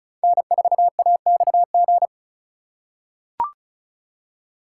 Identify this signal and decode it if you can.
Morse code, heard as a keyed tone, about 36 wpm, 700 Hz. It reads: N4AXG
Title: USA Call Signs - 5 Characters - 4x Character Spacing - 35wpm